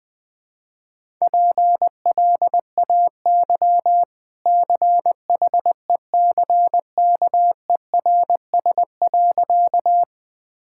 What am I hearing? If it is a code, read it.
Morse code, heard as a keyed tone, about 21 wpm, 700 Hz. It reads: PLAY CHECKERS.
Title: Easy Sentences with Hints - 20wpm